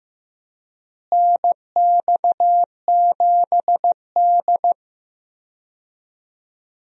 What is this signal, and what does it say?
Morse code, heard as a keyed tone, about 15 wpm, 700 Hz. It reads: NX7D
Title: USA Call Signs - 4 Characters - Repeated 2x - 15wpm